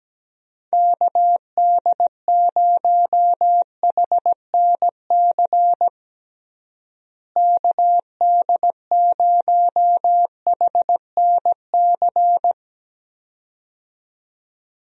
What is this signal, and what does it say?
Morse code, heard as a keyed tone, about 17 wpm, 700 Hz. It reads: KD0HNC KD0HNC
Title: USA Call Signs - 6 Characters - Repeated 2x - 17wpm